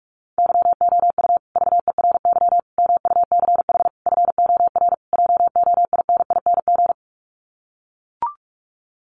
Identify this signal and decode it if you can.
Morse code, heard as a keyed tone, about 44 wpm, 700 Hz. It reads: YOU VERY MUCH FOR JOINING
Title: Sets of 5 Words Encore 45wpm